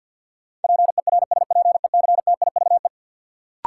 Morse code, heard as a keyed tone, about 53 wpm, 700 Hz. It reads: PERSPECTIVE
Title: Sentences from Top 1200 Words - New Word Drill - 50wpm